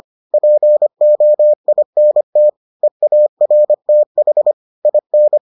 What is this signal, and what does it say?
Morse code, heard as a keyed tone, about 25 wpm, 600 Hz. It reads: POINT EARTH IN